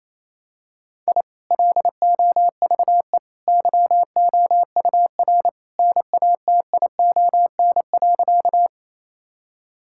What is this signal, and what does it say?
Morse code, heard as a keyed tone, about 28 wpm, 700 Hz. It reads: I LOVE YOUR NATION.
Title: Sentences from Top 1000 Words - Review Part 6 - 28wpm